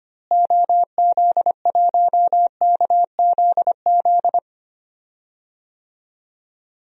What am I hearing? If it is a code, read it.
Morse code, heard as a keyed tone, about 25 wpm, 700 Hz. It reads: OZ1KZZ
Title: DXCC-100 Call Signs - 6 Characters - Repeated 2x - 25wpm